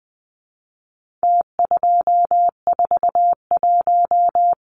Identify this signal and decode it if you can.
Morse code, heard as a keyed tone, about 20 wpm, 700 Hz. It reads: T241
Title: Numbers - Contest Style - 4-Digit 20wpm